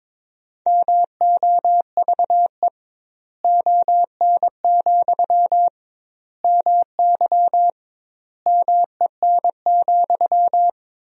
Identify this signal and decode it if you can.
Morse code, heard as a keyed tone, about 22 wpm, 700 Hz. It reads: MOVE ON, MY MEN,
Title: 2x Word-Spacing - Sentences from Top 500 Words - 22wpm